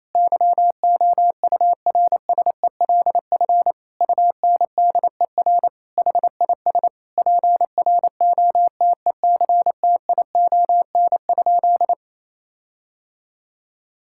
Morse code, heard as a keyed tone, about 28 wpm, 700 Hz. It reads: YOURSELF UNDER HIS PROTECTION?
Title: Sentences from Top 1600 Words 28wpm